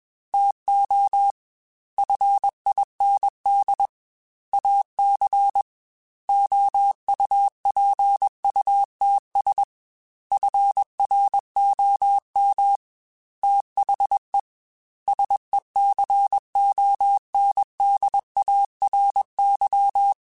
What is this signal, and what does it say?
Morse code, heard as a keyed tone, about 21 wpm, 800 Hz. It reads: TO FIND AC OUPUTS FROM THE SECONDARY
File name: Technical Subjects / Technical Subjects-mp3s / Sw Pwr Sup @20